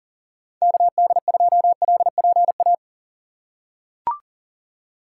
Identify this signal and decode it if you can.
Morse code, heard as a keyed tone, about 39 wpm, 700 Hz. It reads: KD2LPA
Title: USA Call Signs - 6 Characters - 4x Character Spacing - 40wpm